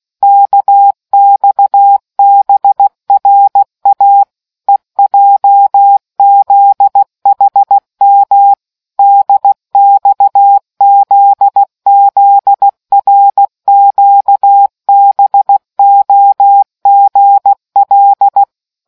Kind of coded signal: Morse code